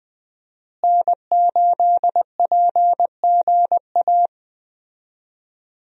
Morse code, heard as a keyed tone, about 21 wpm, 700 Hz. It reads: N8PGA
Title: USA Call Signs - 5 Characters - Repeated 2x - 20wpm